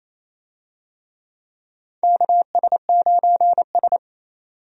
Morse code, heard as a keyed tone, about 28 wpm, 700 Hz. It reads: KS9S
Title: USA Call Signs - 4 Characters - 28wpm